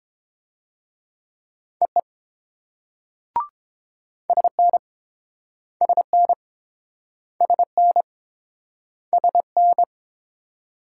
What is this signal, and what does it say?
Morse code, heard as a keyed tone, about 29 wpm, 700 Hz. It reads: EE SN SN SN SN